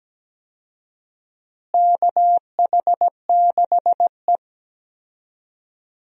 Morse code, heard as a keyed tone, about 17 wpm, 700 Hz. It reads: KH6E
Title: DXCC-100 Call Signs - 4 Characters - Repeated 2x - 17wpm